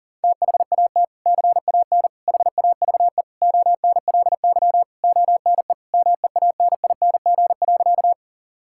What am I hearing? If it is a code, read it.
Morse code, heard as a keyed tone, about 37 wpm, 700 Hz. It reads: THAT CAN HAVE ONLY ONE MEANING.
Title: Sentences from Top 1000 Words - Review Part 6 - 40wpm